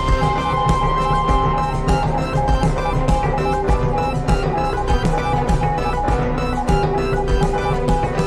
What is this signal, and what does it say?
Morse code, heard as a keyed tone, about 20 wpm, 820 Hz. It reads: CONNECTED. REST